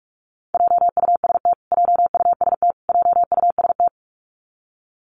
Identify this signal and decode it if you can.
Morse code, heard as a keyed tone, about 44 wpm, 700 Hz. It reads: JUST JUST JUST
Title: Top 300 Words Repeated 45wpm